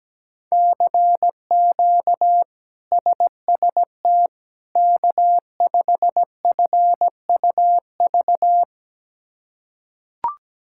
Morse code, heard as a keyed tone, about 17 wpm, 700 Hz. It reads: CQ SST K5FUV
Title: K1USN SST QSO Practice - 17wpm